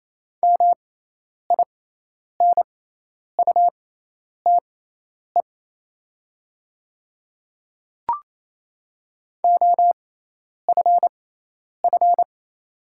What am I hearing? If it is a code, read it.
Morse code, heard as a keyed tone, about 27 wpm, spaced as for 10 wpm, 690 Hz. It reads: MINUTE OFF
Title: 6x Character Spacing - Top 500 Words - 28wpm